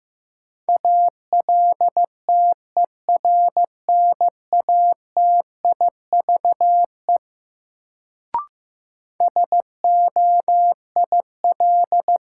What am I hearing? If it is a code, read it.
Morse code, heard as a keyed tone, about 15 wpm, 700 Hz. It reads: ALTERNATIVE SOIL
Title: Sentences from Top 1000 Words - Review Part 4 - New Word Drill - 15wpm